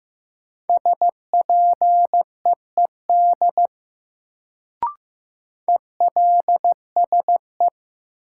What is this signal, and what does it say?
Morse code, heard as a keyed tone, about 16 wpm, 700 Hz. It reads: SPEED ELSE
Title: Sentences from Top 800 Words - New Word Drill - 15wpm